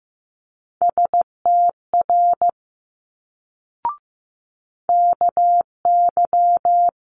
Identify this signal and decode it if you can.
Morse code, heard as a keyed tone, about 15 wpm, 700 Hz. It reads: STR KY
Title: Multi-letter Phonemes Quick Fire 15wpm